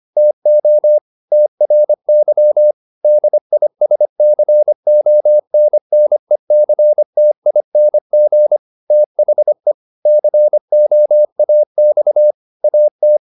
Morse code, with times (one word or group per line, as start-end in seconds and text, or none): 0.2-1.0 TO
1.3-2.7 TRY
3.0-8.6 DISCONNECTING
8.9-9.7 THE
10.1-12.3 COAX
12.6-13.2 AT